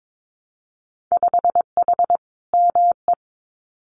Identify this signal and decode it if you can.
Morse code, heard as a keyed tone, about 22 wpm, 700 Hz. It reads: 5H ME
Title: Winter Field Day Exchanges Encore 22wpm